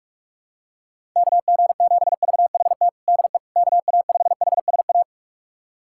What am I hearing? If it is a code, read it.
Morse code, heard as a keyed tone, about 41 wpm, 700 Hz. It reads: KG7VHT DE KA5HSA
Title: USA Callsign DE Callsign - 6 Characters - 45wpm